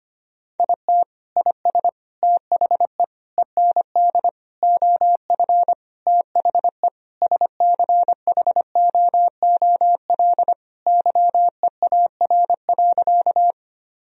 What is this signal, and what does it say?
Morse code, heard as a keyed tone, about 25 wpm, 700 Hz. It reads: IT IS THE END OF THE SCHOOL YEAR.
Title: Sentences from Top 1000 Words - Review Part 1 - 25wpm